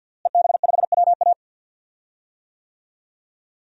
Morse code, heard as a keyed tone, about 52 wpm, 700 Hz. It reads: EB5WA
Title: DXCC-100 Call Signs - 5 Characters - Repeated 3x - 50wpm